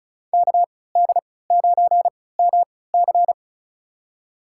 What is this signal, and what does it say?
Morse code, heard as a keyed tone, about 36 wpm, spaced as for 20 wpm, 700 Hz. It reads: KD9MC
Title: USA Call Signs - 5 Characters - 3x Character Spacing - 35wpm